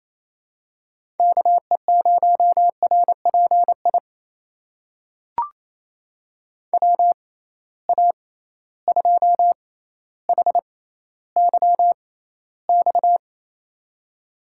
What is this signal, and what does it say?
Morse code, heard as a keyed tone, about 28 wpm, spaced as for 14 wpm, 700 Hz. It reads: KE0RPI WA2HYX
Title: USA Call Signs - 6 Characters - 6x Character Spacing - 28wpm